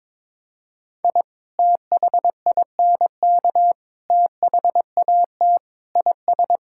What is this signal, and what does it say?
Morse code, heard as a keyed tone, about 21 wpm, 700 Hz. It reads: I THINK THAT IS
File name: practice / 22wpm / Sentences with 3-Word Phrases - 22wpm